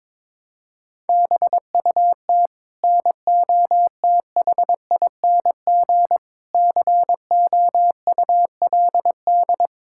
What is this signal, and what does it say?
Morse code, heard as a keyed tone, about 22 wpm, 700 Hz. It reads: BUT NOTHING COULD
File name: practice / 22wpm / Sets of 8 Words - 22wpm